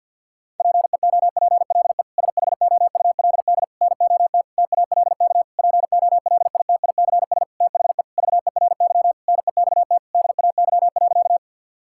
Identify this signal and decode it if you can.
Morse code, heard as a keyed tone, about 50 wpm, 700 Hz. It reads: PEOPLE SHOULD NOT TALK POLITICS THE VERY NEXT DAY.
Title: Sentences from Top 1000 Words - Review Part 6 - 50wpm